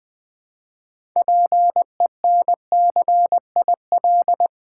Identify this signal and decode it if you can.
Morse code, heard as a keyed tone, about 20 wpm, 700 Hz. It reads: PENCIL
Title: 6 Letter Words - First 3 Letters Given - 20wpm